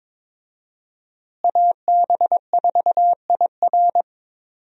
Morse code, heard as a keyed tone, about 22 wpm, 700 Hz. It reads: AB4IR
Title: USA Call Signs - 5 Characters - Repeated 2x - 22wpm